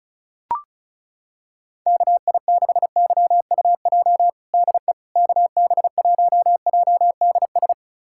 Morse code, heard as a keyed tone, about 36 wpm, 700 Hz. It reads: KI6YUJ DE KB1JDS
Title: USA Callsign DE Callsign 35wpm